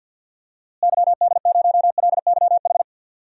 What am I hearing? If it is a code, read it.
Morse code, heard as a keyed tone, about 52 wpm, 700 Hz. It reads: YD0LYH
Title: DXCC-100 Call Signs - 6 Characters - Repeated 2x - 50wpm